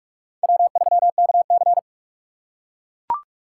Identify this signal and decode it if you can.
Morse code, heard as a keyed tone, about 42 wpm, 700 Hz. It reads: W3KC